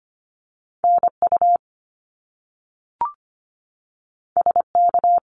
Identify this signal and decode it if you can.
Morse code, heard as a keyed tone, about 25 wpm, 700 Hz. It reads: NU SK
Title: Canadian Provinces and Territories 25wpm